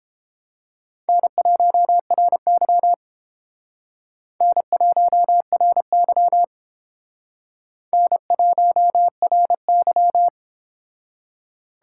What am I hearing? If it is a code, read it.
Morse code, heard as a keyed tone, about 30 wpm, 700 Hz. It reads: N1RY N1RY N1RY